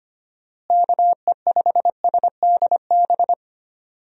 Morse code, heard as a keyed tone, about 25 wpm, 700 Hz. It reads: KE5SDB
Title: USA Call Signs - 6 Characters - Repeated 2x - 25wpm